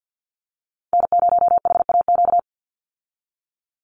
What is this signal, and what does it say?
Morse code, heard as a keyed tone, about 50 wpm, 700 Hz. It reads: N0HAQ
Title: USA Call Signs 50wpm